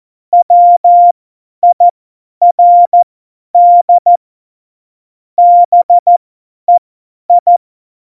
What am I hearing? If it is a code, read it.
Morse code, heard as a keyed tone, about 14 wpm, spaced as for 10 wpm, 700 Hz. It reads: WIRD BEI
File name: Morsekurs / Witze / Witz 21. Mai 2019, 50 F70